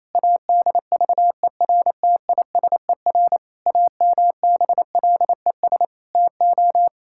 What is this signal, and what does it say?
Morse code, heard as a keyed tone, about 28 wpm, 700 Hz. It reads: ADVERTISER AMBLES TO
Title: Pangram Challenge v2 - 28wpm